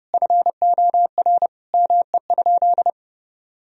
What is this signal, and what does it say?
Morse code, heard as a keyed tone, about 29 wpm, 700 Hz. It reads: FOR ME?
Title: Sentences from Top 1000 Words - Review Part 3 - 30wpm